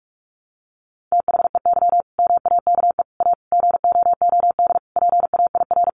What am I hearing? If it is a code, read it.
Morse code, heard as a keyed tone, about 44 wpm, 700 Hz. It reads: THEY MAKE A GOOD PAIR
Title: Sentences from Top 1200 Words 45wpm